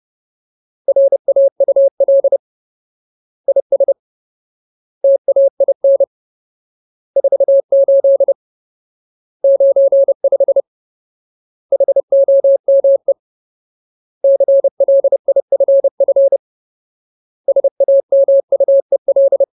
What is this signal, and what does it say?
Morse code, heard as a keyed tone, about 30 wpm, 550 Hz. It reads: RAUL IS TAIN 48 95 HOME CLIFF SAMUEL
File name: Ultimate Word List / 30WPM Ultimate